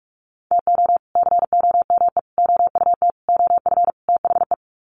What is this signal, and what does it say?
Morse code, heard as a keyed tone, about 44 wpm, 700 Hz. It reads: TO COME OUT OF THE
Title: Sets of 5 Words Encore2 45wpm